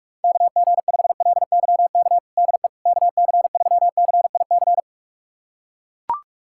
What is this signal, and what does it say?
Morse code, heard as a keyed tone, about 43 wpm, 700 Hz. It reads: KG5LYK DE KC3CIC
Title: USA Callsign DE Callsign - 6 Characters - 45wpm